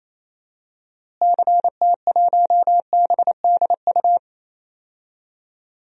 Morse code, heard as a keyed tone, about 28 wpm, 700 Hz. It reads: CT1BDU
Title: DXCC-100 Call Signs - 6 Characters - Repeated 2x - 28wpm